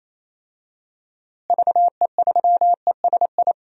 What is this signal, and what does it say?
Morse code, heard as a keyed tone, about 27 wpm, 700 Hz. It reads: VE3ESI
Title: DXCC-100 Call Signs - 6 Characters - Repeated 2x - 28wpm